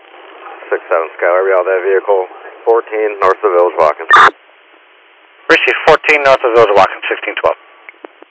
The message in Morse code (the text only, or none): none